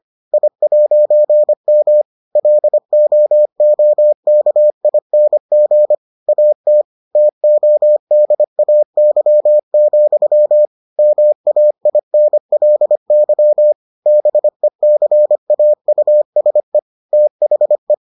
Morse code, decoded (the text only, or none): I'M LOOKING AT TODAY, MAINLY BECAUSE THE